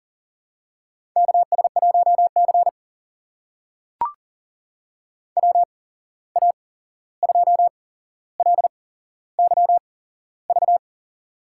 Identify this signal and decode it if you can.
Morse code, heard as a keyed tone, about 40 wpm, spaced as for 15 wpm, 700 Hz. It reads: KS1C WA2LYV
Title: USA Call Signs - 8x Character Spacing - 40wpm